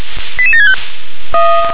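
Five-tone selective call (ZVEI), heard as quick runs of five tones, tones of about 80 ms. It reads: 88765
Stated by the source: ZVEI five-tone selective call received on an SDR.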